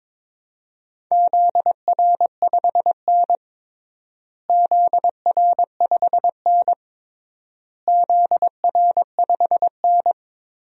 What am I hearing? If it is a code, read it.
Morse code, heard as a keyed tone, about 22 wpm, 700 Hz. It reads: ZR5N ZR5N ZR5N
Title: DXCC-100 Call Signs - 4 Characters - Repeated 3x - 22wpm